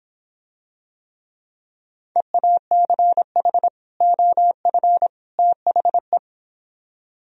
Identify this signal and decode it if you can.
Morse code, heard as a keyed tone, about 26 wpm, 700 Hz. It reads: EACH OF THE